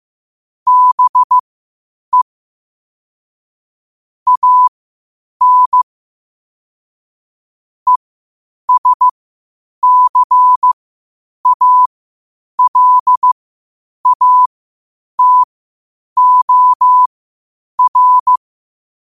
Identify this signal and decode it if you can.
Morse code, heard as a keyed tone, about 15 wpm, spaced as for 8 wpm, 1000 Hz. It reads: BE AN ESCALATOR
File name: qotdfiles / 8wpm